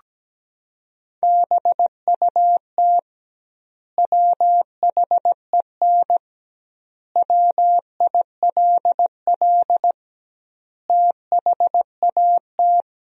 Morse code, heard as a keyed tone, about 17 wpm, 700 Hz. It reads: BUT WHEN WILL THAT
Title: 2x Word-Spacing - Sentences from Top 100 Words - 17wpm